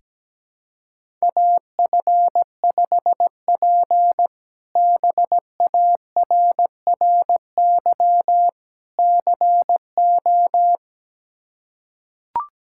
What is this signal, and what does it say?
Morse code, heard as a keyed tone, about 17 wpm, 700 Hz. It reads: AF5P BARRY CO
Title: K1USN SST QSO Practice - 17wpm